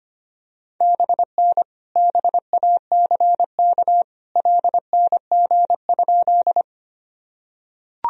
Morse code, heard as a keyed tone, about 25 wpm, 700 Hz. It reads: BN BACK LNG?
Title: Sentences from Top 200 Words and Abbreviations 25wpm